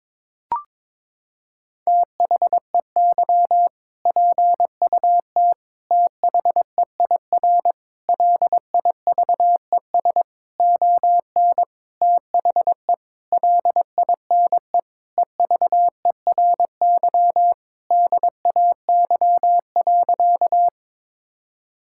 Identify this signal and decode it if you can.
Morse code, heard as a keyed tone, about 22 wpm, 700 Hz. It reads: THEY PUT THEIR LIVES ON THE LINE EVERY DAY.
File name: practice / 22wpm / Sentences with 8-Word Phrases - 22wpm